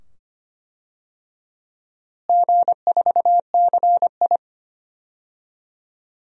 Morse code, heard as a keyed tone, about 25 wpm, 700 Hz. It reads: G4CI
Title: DXCC-100 Call Signs - 4 Characters - Repeated 2x - 25wpm